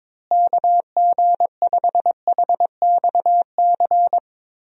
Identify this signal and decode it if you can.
Morse code, heard as a keyed tone, about 23 wpm, 700 Hz. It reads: KG5HXC
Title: USA Call Signs - 6 Characters - Repeated 3x - 22wpm